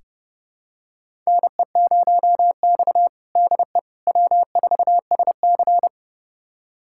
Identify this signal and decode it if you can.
Morse code, heard as a keyed tone, about 30 wpm, 700 Hz. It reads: NE0X DE W4SC
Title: USA Callsign DE Callsign - 4 Characters - 30wpm